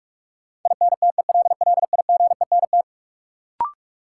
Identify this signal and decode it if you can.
Morse code, heard as a keyed tone, about 43 wpm, 700 Hz. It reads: INTELLIGENT
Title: Sentences from Top 1000 Words - Review Part 7 - New Word Drill - 45wpm